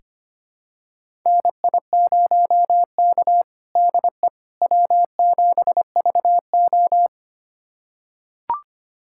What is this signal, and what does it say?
Morse code, heard as a keyed tone, about 25 wpm, 700 Hz. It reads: NI0K DE W7VO
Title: USA Callsign DE Callsign - 4 Characters - 25wpm